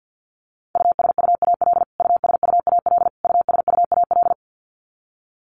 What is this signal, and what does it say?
Morse code, heard as a keyed tone, about 50 wpm, 700 Hz. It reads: USUAL USUAL USUAL
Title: Top 500 Words Repeated 50wpm